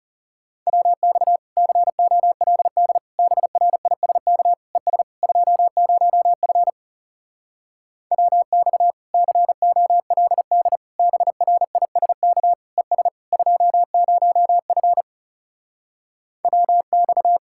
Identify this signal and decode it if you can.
Morse code, heard as a keyed tone, about 40 wpm, 700 Hz. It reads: WX COLD BRISK ES 20F WX COLD BRISK ES 20F WX